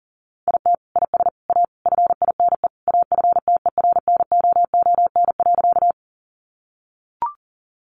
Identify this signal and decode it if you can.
Morse code, heard as a keyed tone, about 40 wpm, 700 Hz. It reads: IT IS A FINE AFTERNOON.
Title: Sentences from Top 1500 Words 40wpm